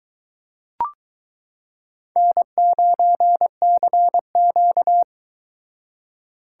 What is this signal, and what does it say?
Morse code, heard as a keyed tone, about 22 wpm, 700 Hz. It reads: N9CQ